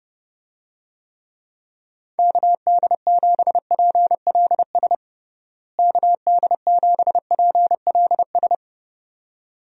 Morse code, heard as a keyed tone, about 30 wpm, 700 Hz. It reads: KD7PLS KD7PLS
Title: USA Call Signs - 6 Characters - Repeated 2x - 30wpm